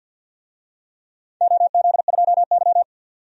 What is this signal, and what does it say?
Morse code, heard as a keyed tone, about 52 wpm, 700 Hz. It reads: OZ2Y
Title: DXCC-100 Call Signs - 4 Characters - 50wpm